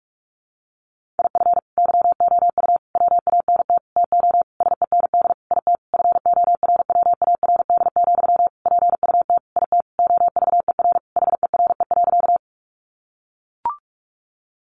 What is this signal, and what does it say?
Morse code, heard as a keyed tone, about 44 wpm, 700 Hz. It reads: IF YOU WANT TO SEND IT FORWARD, PUT IT OVER HERE.
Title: Sentences from Top 1500 Words 45wpm